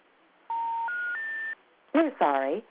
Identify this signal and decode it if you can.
Special information tone, with speech.
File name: things / first dial 1 or 0